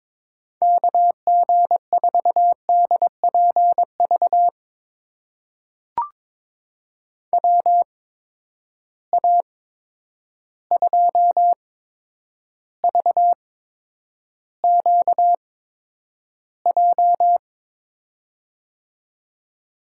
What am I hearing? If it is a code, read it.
Morse code, heard as a keyed tone, about 22 wpm, spaced as for 9 wpm, 700 Hz. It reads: KG4DPV WA2VQJ
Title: USA Call Signs - 6 Characters - 8x Character Spacing - 22wpm